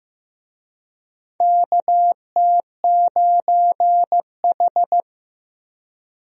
Morse code, heard as a keyed tone, about 15 wpm, 700 Hz. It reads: KT9H
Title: USA Call Signs - 4 Characters - Repeated 3x - 15wpm